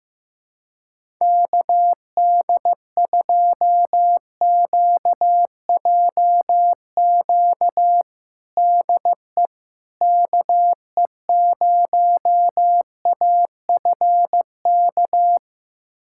Morse code, heard as a keyed tone, about 15 wpm, 700 Hz. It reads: KD2QJQ DE KE0AFK
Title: USA Callsign DE Callsign 15wpm